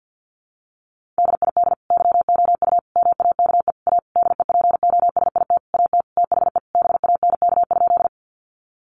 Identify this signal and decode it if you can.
Morse code, heard as a keyed tone, about 50 wpm, 700 Hz. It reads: DID YOU MAKE A DEPOSIT AT THE BANK?
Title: Sentences from Top 1300 Words 50wpm